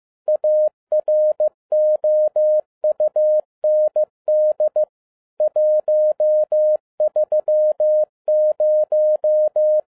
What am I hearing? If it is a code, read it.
Morse code, heard as a keyed tone, about 15 wpm, 610 Hz. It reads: AROUND 130
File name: 2011-07-mp3 / n47 JOHN K3TN 15wpm